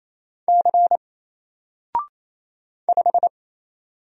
Morse code, heard as a keyed tone, about 27 wpm, 700 Hz. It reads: C 5
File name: Single-Letter-Number-28wpm